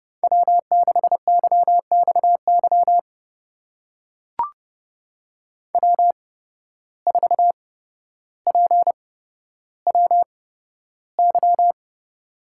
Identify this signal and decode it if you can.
Morse code, heard as a keyed tone, about 30 wpm, spaced as for 13 wpm, 700 Hz. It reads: W6YXY W4PWY
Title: USA Call Signs - 5 Characters - 8x Character Spacing - 30wpm